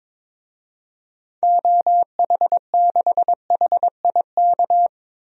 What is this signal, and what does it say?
Morse code, heard as a keyed tone, about 22 wpm, 700 Hz. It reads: OH6HIK
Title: DXCC-100 Call Signs - 6 Characters - 22wpm